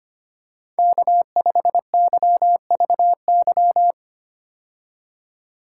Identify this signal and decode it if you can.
Morse code, heard as a keyed tone, about 25 wpm, 700 Hz. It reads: K5YVY
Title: K1USN SST QSO Practice - 25wpm